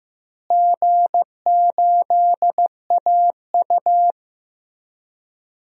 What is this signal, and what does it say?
Morse code, heard as a keyed tone, about 16 wpm, 700 Hz. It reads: G8AU
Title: DXCC-100 Call Signs - 4 Characters - Repeated 2x - 15wpm